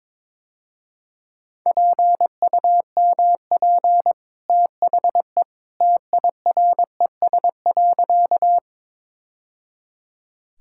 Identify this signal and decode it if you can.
Morse code, heard as a keyed tone, about 22 wpm, 700 Hz. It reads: PUMP THE TIRES.
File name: practice / 22wpm / Easy Sentences with Hints - 22wpm